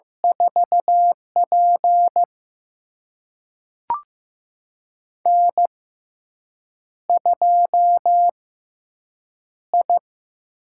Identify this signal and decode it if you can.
Morse code, heard as a keyed tone, about 15 wpm, spaced as for 7 wpm, 700 Hz. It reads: 4P N2I
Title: USA Call Signs - 4 Characters - 6x Character Spacing - 15wpm